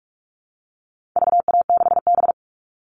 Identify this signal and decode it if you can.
Morse code, heard as a keyed tone, about 44 wpm, 700 Hz. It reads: VA6B
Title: All Cut Numbers - 4-Digit 45wpm